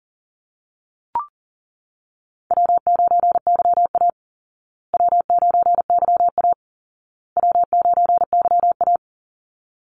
Morse code, heard as a keyed tone, about 37 wpm, 700 Hz. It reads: W9YA W9YA W9YA
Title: Call Signs Repeated 2x - 40wpm